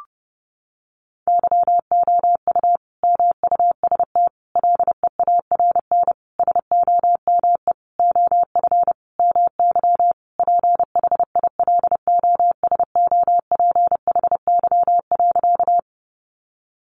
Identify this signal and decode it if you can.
Morse code, heard as a keyed tone, about 30 wpm, 700 Hz. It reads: YOU MUST LEARN SOME OF MY PHILOSOPHY.
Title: Sentences from Top 900 Words 30wpm